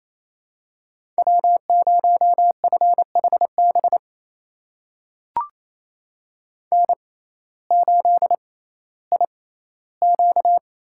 Morse code, heard as a keyed tone, about 28 wpm, spaced as for 15 wpm, 700 Hz. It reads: W0FHB N8IQ
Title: USA Call Signs - 5 Characters - 6x Character Spacing - 28wpm